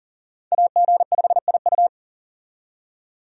Morse code, heard as a keyed tone, about 40 wpm, 700 Hz. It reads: AG5IU
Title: USA Call Signs - 5 Characters - 40wpm